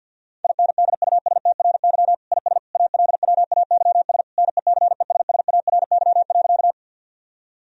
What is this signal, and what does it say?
Morse code, heard as a keyed tone, about 50 wpm, 700 Hz. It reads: INDUSTRY IS ALWAYS NECESSARY.
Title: Sentences from Top 1000 Words - Review Part 4 - 50wpm